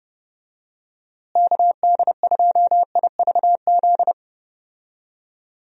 Morse code, heard as a keyed tone, about 30 wpm, 700 Hz. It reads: KD2IVZ
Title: USA Call Signs - 6 Characters - Repeated 2x - 30wpm